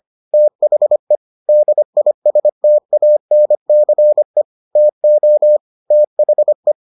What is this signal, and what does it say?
Morse code, heard as a keyed tone, about 25 wpm, 600 Hz. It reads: THE DISTANCE TO THE